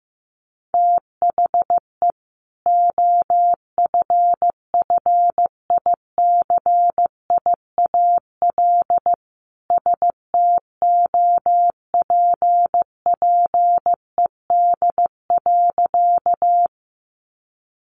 Morse code, heard as a keyed tone, about 15 wpm, 700 Hz. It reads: THE OFFICIAL STOPPED.
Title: A Sound Of Thunder 15wpm